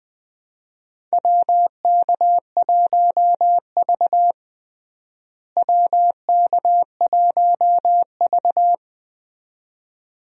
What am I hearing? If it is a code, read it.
Morse code, heard as a keyed tone, about 20 wpm, 700 Hz. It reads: WK1V WK1V
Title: USA Call Signs - 4 Characters - Repeated 2x - 20wpm